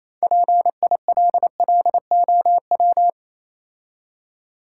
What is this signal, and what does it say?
Morse code, heard as a keyed tone, about 28 wpm, 690 Hz. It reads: PILLOW